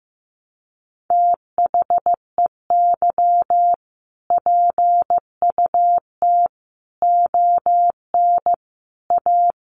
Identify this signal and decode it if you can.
Morse code, heard as a keyed tone, about 15 wpm, 700 Hz. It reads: THEY PUT ON A
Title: Sentences from Top 1200 Words 15wpm